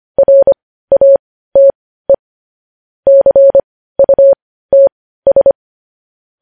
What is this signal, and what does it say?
Morse code, heard as a keyed tone, about 25 wpm, spaced as for 15 wpm, 550 Hz. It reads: RATE CUTS
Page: News headlines in morse code at 15 wpm